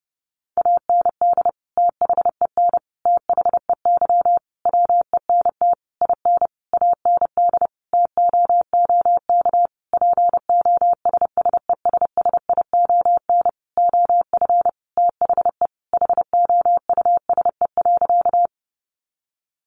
Morse code, heard as a keyed tone, about 30 wpm, 700 Hz. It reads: AND THEN THEY WENT IN AND TOOK POSSESSION OF THE HOUSE.
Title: Sentences from Top 1400 Words 30wpm